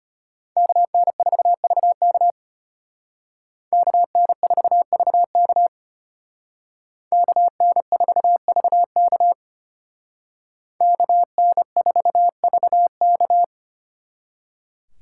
Morse code, heard as a keyed tone, about 33 wpm, 700 Hz. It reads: KN4VK KN4VK KN4VK KN4VK